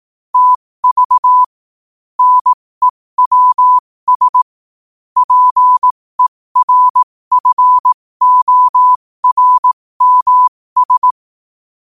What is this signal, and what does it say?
Morse code, heard as a keyed tone, about 18 wpm, spaced as for 15 wpm, 1000 Hz. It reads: TV NEWS PERFORMS
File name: qotdfiles / 15wpm